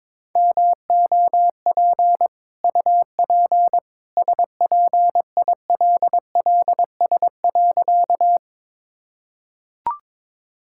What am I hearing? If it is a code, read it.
Morse code, heard as a keyed tone, about 21 wpm, 690 Hz. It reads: MOP UP SPILLS.
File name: practice / 22wpm / Easy Sentences with Hints - 22wpm